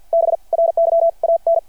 Morse code, heard as a keyed tone, about 51 wpm, spaced as for 40 wpm, 660 Hz. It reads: BRYAN
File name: archived / CRSnet